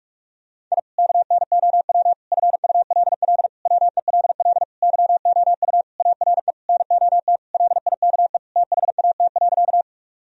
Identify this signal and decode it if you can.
Morse code, heard as a keyed tone, about 41 wpm, 700 Hz. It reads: I KNOW FULL WELL YOU ARE NOT LIKE THAT.
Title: Sentences from Top 1000 Words - Review Part 3 - 45wpm